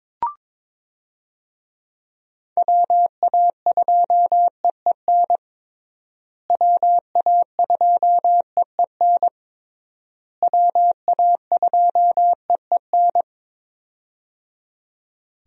Morse code, heard as a keyed tone, about 22 wpm, 700 Hz. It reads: WA2EEN WA2EEN WA2EEN
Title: USA Call Signs - 6 Characters - Repeated 3x - 22wpm